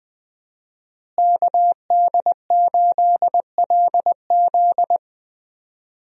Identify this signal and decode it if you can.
Morse code, heard as a keyed tone, about 20 wpm, 700 Hz. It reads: KD8LZ
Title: USA Call Signs - 5 Characters - Repeated 3x - 20wpm